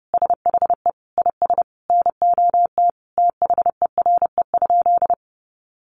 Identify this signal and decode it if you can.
Morse code, heard as a keyed tone, about 30 wpm, 700 Hz. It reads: SHE IS NOT THERE?
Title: Sentences from Top 600 Words 30wpm